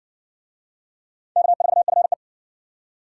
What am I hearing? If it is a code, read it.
Morse code, heard as a keyed tone, about 59 wpm, 700 Hz. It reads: K4FE